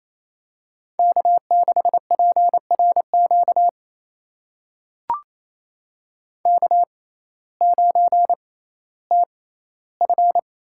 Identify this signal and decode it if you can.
Morse code, heard as a keyed tone, about 28 wpm, spaced as for 15 wpm, 700 Hz. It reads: K6PRQ K9TF
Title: USA Call Signs - 5 Characters - 6x Character Spacing - 28wpm